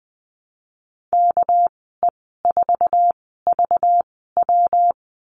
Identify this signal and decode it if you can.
Morse code, heard as a keyed tone, about 20 wpm, spaced as for 15 wpm, 700 Hz. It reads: KE4VW
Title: 207 Lesson 33g - Letter j - Call Signs 20wpm